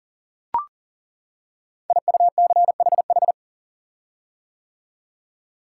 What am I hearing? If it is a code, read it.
Morse code, heard as a keyed tone, about 38 wpm, 700 Hz. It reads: IUCHH
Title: Code Groups - 5 Letters - 40wpm